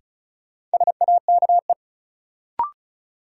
Morse code, heard as a keyed tone, about 36 wpm, 700 Hz. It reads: SAKE
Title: Sets of 4 Words - New Word Drill - 35wpm